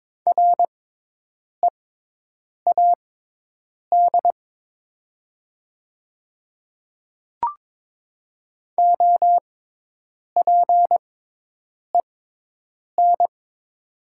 Morse code, heard as a keyed tone, about 22 wpm, spaced as for 8 wpm, 700 Hz. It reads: READ OPEN
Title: 6x Character Spacing - Top 300 Words - 22wpm